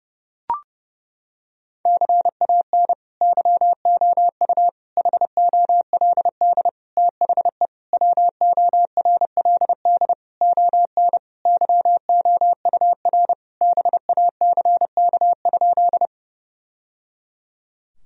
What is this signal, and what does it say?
Morse code, heard as a keyed tone, about 32 wpm, 700 Hz. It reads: CAN YOU HOLD THE WORLD ON YOUR BACK?
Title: Sentences from Top 1000 Words - Review Part 2 - 30wpm